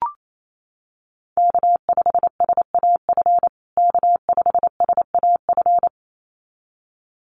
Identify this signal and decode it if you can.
Morse code, heard as a keyed tone, about 28 wpm, 700 Hz. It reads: K5SAF K5SAF
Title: USA Call Signs Repeated 2x 28wpm